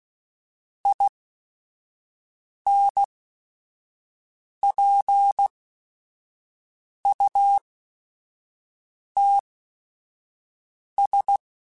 Morse code, heard as a keyed tone, about 16 wpm, spaced as for 5 wpm, 800 Hz. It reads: INPUTS
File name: Technical Subjects / Technical Subjects-mp3s / AND Gate @5